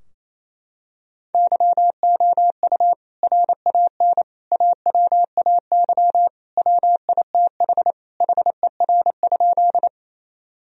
Morse code, heard as a keyed tone, about 28 wpm, 700 Hz. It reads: YOU RAN AWAY WITH HER?
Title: Sentences from Top 1000 Words - Review Part 3 - 28wpm